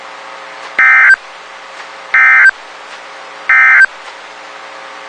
Emergency Alert System (SAME) tones, heard as harsh buzzing bursts. It is heard from 0.7 to 3.9 s.
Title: WXK91 NOAA Weather Radio